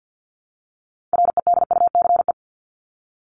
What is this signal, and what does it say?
Morse code, heard as a keyed tone, about 50 wpm, 700 Hz. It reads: REDUCE
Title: ED Suffix - Top 100 Words with Base Words 50wpm